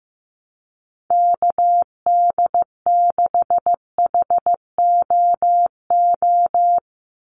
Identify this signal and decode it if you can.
Morse code, heard as a keyed tone, about 15 wpm, 700 Hz. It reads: KD6HOO
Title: USA Call Signs 15wpm